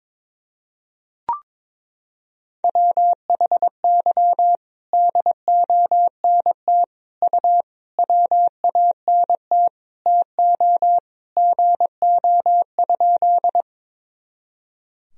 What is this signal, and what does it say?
Morse code, heard as a keyed tone, about 22 wpm, 700 Hz. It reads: WHY DONT U WANT TO GO?
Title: Sentences from Top 300 Words and Abbreviations 22wpm